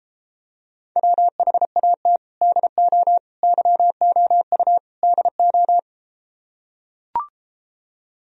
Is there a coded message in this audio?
Yes: Morse code from 1.0 to 5.8 s, heard as a keyed tone, about 33 wpm, 700 Hz. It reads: WHAT DO YOU DO